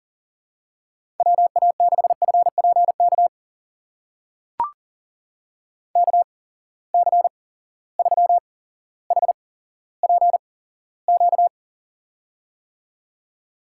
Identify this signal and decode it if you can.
Morse code, heard as a keyed tone, about 40 wpm, spaced as for 17 wpm, 700 Hz. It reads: WA6FPK KC3HPQ
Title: USA Call Signs - 6 Characters - 8x Character Spacing - 40wpm